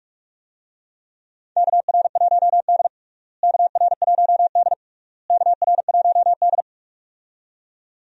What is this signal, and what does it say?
Morse code, heard as a keyed tone, about 46 wpm, 700 Hz. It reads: KR1D KR1D KR1D
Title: USA Call Signs - 4 Characters - Repeated 3x - 45wpm